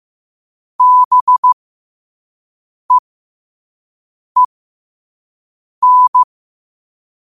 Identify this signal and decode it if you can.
Morse code, heard as a keyed tone, about 15 wpm, spaced as for 5 wpm, 1000 Hz. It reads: BEEN
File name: 5wpm